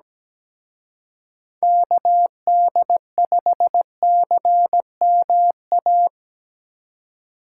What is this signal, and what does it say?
Morse code, heard as a keyed tone, about 20 wpm, 700 Hz. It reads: KD5CMA
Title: Speed Racing - USA 2x3 Call Signs 17wpm